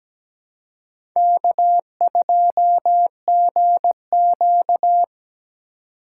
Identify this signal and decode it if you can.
Morse code, heard as a keyed tone, about 20 wpm, 700 Hz. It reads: K2GQ
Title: Speed Racing - USA 1x2 Call Signs 17wpm